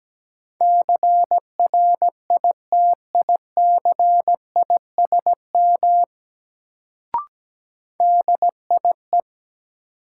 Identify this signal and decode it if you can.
Morse code, heard as a keyed tone, about 17 wpm, 700 Hz. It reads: CRITICISM DIE
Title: Sentences from Top 1300 Words - New Word Drill - 17wpm